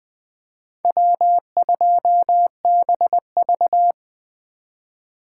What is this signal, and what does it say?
Morse code, heard as a keyed tone, about 20 wpm, 700 Hz. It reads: W2BV
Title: Four Letter-Number Combinations - 20wpm